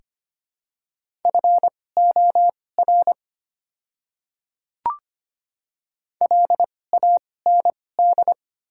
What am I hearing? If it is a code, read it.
Morse code, heard as a keyed tone, about 25 wpm, spaced as for 18 wpm, 700 Hz. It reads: FOR LAND
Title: 2x Character Spacing - Top 200 Words - 25wpm